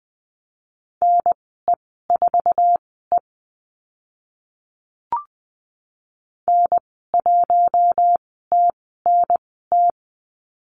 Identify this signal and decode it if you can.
Morse code, heard as a keyed tone, about 20 wpm, spaced as for 14 wpm, 700 Hz. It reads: NE4E N1TNT
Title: 062 Lesson 11g - Letter h - Call Signs 20wpm